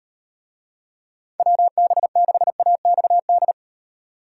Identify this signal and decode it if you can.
Morse code, heard as a keyed tone, about 38 wpm, 700 Hz. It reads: WB6AXD